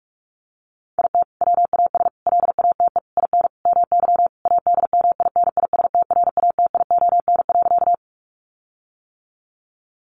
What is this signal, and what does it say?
Morse code, heard as a keyed tone, about 44 wpm, 700 Hz. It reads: IT WAS LATE IN MY ADMINISTRATION.
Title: Sentences from Top 1300 Words 45wpm